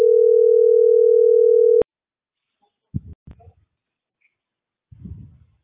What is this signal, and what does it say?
Ringback tone (North America), 440 + 480 Hz.